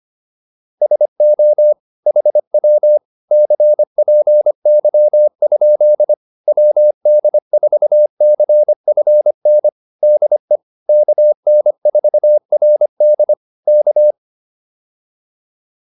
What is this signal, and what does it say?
Morse code, heard as a keyed tone, about 25 wpm, 600 Hz. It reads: SO HW CPY? WD4CFN DE KN4RD K